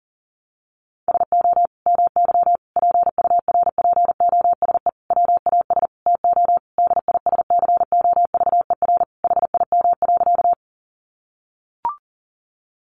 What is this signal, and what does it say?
Morse code, heard as a keyed tone, about 40 wpm, 700 Hz. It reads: SO MY PURPOSE WAS TO DISCOVER HIM.
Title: Sentences from Top 1100 Words 40wpm